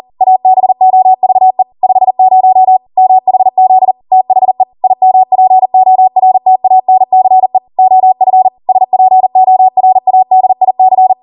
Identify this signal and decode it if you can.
Morse code, heard as a keyed tone, about 40 wpm, 750 Hz. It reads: ABOVE 50 MHZ THE IMPORTANCE OF SPORADIC